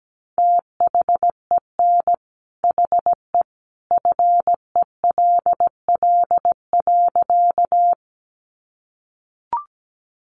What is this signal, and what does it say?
Morse code, heard as a keyed tone, about 17 wpm, 700 Hz. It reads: THEN HE FELL.
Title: A Sound Of Thunder 17wpm